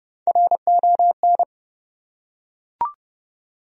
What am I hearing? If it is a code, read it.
Morse code, heard as a keyed tone, about 30 wpm, 700 Hz. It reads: RON
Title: USA Common Names - Max 5 Letters - 30wpm